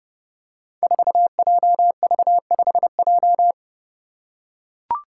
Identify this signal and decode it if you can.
Morse code, heard as a keyed tone, about 29 wpm, 700 Hz. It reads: 4JV5J
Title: Five Challenging Letter-Number Combinations 30wpm